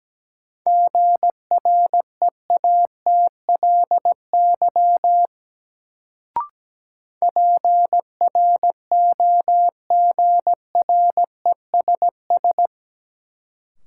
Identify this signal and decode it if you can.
Morse code, heard as a keyed tone, about 17 wpm, 700 Hz. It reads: GREATLY PROGRESS
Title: Sentences from Top 1400 Words - New Word Drill - 17wpm